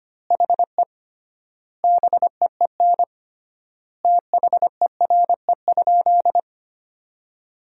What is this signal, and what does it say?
Morse code, heard as a keyed tone, about 25 wpm, 700 Hz. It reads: HE BEEN THERE?
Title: 3x Word-Spacing - Sentences from Top 100 Words - 25wpm